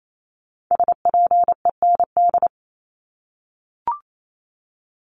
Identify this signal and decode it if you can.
Morse code, heard as a keyed tone, about 28 wpm, 700 Hz. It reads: SPEND
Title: Top 501-750 Words 28wpm